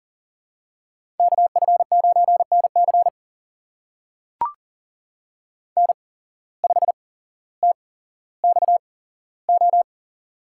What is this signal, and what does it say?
Morse code, heard as a keyed tone, about 40 wpm, spaced as for 17 wpm, 700 Hz. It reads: KF9NC N5TXO
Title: USA Call Signs - 5 Characters - 8x Character Spacing - 40wpm